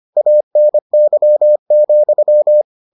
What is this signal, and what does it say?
Morse code, heard as a keyed tone, about 25 wpm, 600 Hz. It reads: ANY,